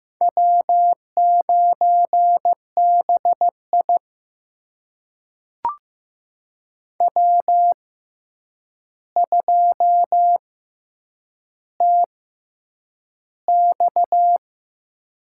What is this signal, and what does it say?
Morse code, heard as a keyed tone, about 15 wpm, spaced as for 8 wpm, 700 Hz. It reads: W9BI W2TX
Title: USA Call Signs - 4 Characters - 6x Character Spacing - 15wpm